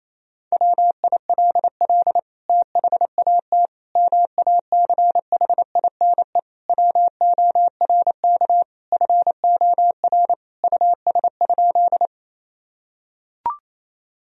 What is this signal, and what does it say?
Morse code, heard as a keyed tone, about 28 wpm, 700 Hz. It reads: WILL THAT MACHINE WORK FOR US?
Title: Sentences from Top 1000 Words - Review Part 2 - 28wpm